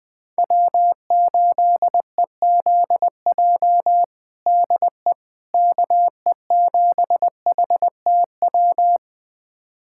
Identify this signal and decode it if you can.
Morse code, heard as a keyed tone, about 20 wpm, 700 Hz. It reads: W8EZJ DE KE7HTW
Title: USA Callsign DE Callsign 20wpm